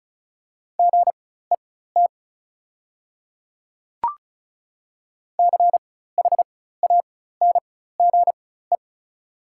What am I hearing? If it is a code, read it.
Morse code, heard as a keyed tone, about 34 wpm, spaced as for 16 wpm, 700 Hz. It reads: GET CHANGE
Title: 4x Character Spacing - Top 200 Words - 35wpm